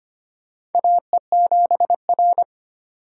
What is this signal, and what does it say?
Morse code, heard as a keyed tone, about 25 wpm, 700 Hz. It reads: AE7R
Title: USA Call Signs - 4 Characters - 6x Character Spacing - 25wpm